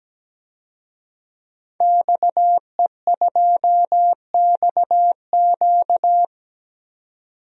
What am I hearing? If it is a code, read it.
Morse code, heard as a keyed tone, about 17 wpm, 700 Hz. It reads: XE2XQ
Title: DXCC-100 Call Signs - 5 Characters - 17wpm